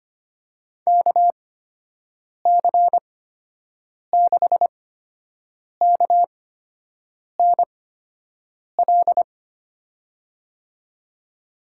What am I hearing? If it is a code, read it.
Morse code, heard as a keyed tone, about 25 wpm, spaced as for 7 wpm, 700 Hz. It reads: KC6KNL
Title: USA Call Signs - 6 Characters - 8x Character Spacing - 25wpm